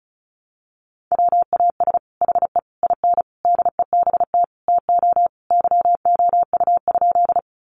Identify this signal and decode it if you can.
Morse code, heard as a keyed tone, about 34 wpm, 700 Hz. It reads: WAS HE IN DEBT TO YOU?
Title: Sentences from Top 1300 Words 35wpm